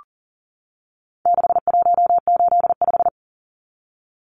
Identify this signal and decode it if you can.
Morse code, heard as a keyed tone, about 38 wpm, 700 Hz. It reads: 6185
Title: Numbers - 4-Digit 40wpm